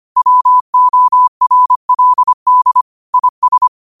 Morse code, heard as a keyed tone, about 25 wpm, 1000 Hz. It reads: WORLD IS
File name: 25wpm